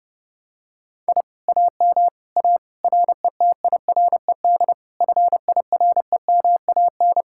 Morse code, heard as a keyed tone, about 29 wpm, 700 Hz. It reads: I AM A RETIRED FIREMAN